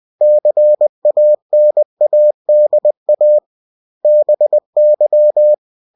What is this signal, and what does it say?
Morse code, heard as a keyed tone, about 20 wpm, 600 Hz. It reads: CANADA BY
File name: cw-books / winnie-the-pooh / winnie-20wpm-page-0111